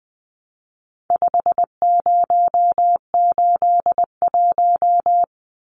Morse code, heard as a keyed tone, about 20 wpm, 700 Hz. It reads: 5081
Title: Numbers - 4-Digit 20wpm